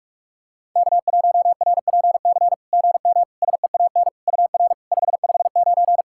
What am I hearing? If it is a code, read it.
Morse code, heard as a keyed tone, about 44 wpm, 690 Hz. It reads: K1RPC GM SEAN UR 559
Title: SOTA QSO Practice 45wpm